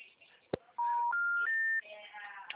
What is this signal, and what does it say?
Special information tone, with noise.